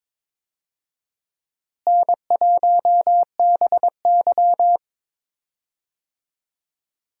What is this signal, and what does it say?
Morse code, heard as a keyed tone, about 22 wpm, 700 Hz. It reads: N1BY
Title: USA Call Signs - 4 Characters - Repeated 2x - 22wpm